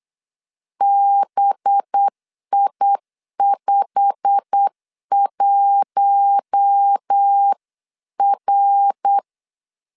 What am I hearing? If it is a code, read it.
Morse code, heard as a keyed tone, about 8 wpm, 800 Hz. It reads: BI51R
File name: LEZIONI / 9 2867 8wpm 3440